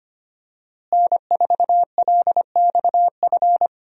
Morse code, heard as a keyed tone, about 25 wpm, 700 Hz. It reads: N4LXF
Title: USA Call Signs - 5 Characters - Repeated 2x - 25wpm